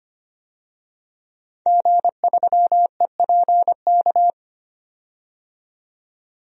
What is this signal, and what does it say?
Morse code, heard as a keyed tone, about 25 wpm, 700 Hz. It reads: G3EPK
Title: DXCC-100 Call Signs - 5 Characters - 25wpm